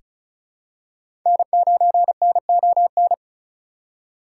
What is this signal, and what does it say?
Morse code, heard as a keyed tone, about 36 wpm, 700 Hz. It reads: N9NON
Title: USA Call Signs - 3x Character Spacing - 35wpm